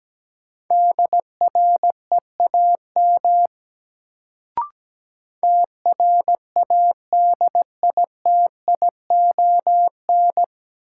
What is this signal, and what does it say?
Morse code, heard as a keyed tone, about 17 wpm, 700 Hz. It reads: DREAM TRADITION
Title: Sentences from Top 1500 Words - New Word Drill - 17wpm